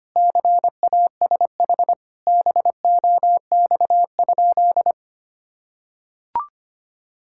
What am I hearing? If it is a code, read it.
Morse code, heard as a keyed tone, about 25 wpm, 700 Hz. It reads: CASH BOX?
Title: Sentences from Top 1000 Words - Review Part 5 - 25wpm